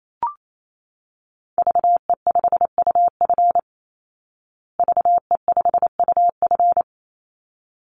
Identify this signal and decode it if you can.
Morse code, heard as a keyed tone, about 28 wpm, 700 Hz. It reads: VE5UF VE5UF
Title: Call Signs Repeated 3x - 28wpm